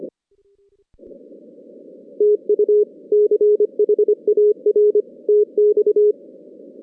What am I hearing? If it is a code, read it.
Morse code, heard as a keyed tone, about 25 wpm, 420 Hz. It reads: TU CHAR TX
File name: 2016 JAN CW NAQP / 2016 Jan naqp k5ot - 0-8